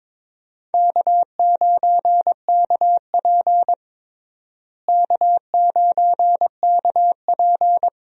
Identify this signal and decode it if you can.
Morse code, heard as a keyed tone, about 22 wpm, 700 Hz. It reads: K9KP K9KP
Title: USA Call Signs - 4 Characters - Repeated 2x - 22wpm